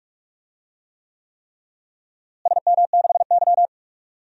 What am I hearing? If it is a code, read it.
Morse code, heard as a keyed tone, about 46 wpm, 700 Hz. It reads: SM6Y
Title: DXCC-100 Call Signs - 4 Characters - Repeated 3x - 45wpm